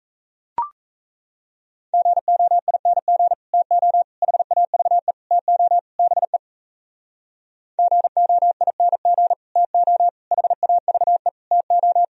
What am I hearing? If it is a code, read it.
Morse code, heard as a keyed tone, about 36 wpm, 700 Hz. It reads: GOING TO HAVE TO BE GOING TO HAVE TO